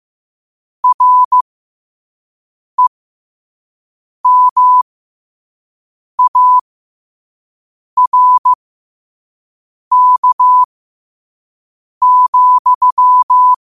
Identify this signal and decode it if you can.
Morse code, heard as a keyed tone, about 15 wpm, spaced as for 5 wpm, 1000 Hz. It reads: REMARK,